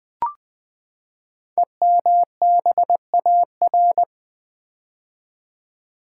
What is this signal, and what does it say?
Morse code, heard as a keyed tone, about 20 wpm, 700 Hz. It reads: EMBAR
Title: Build A Word - 5 Letters At A Time - 12-Letter Words - 20wpm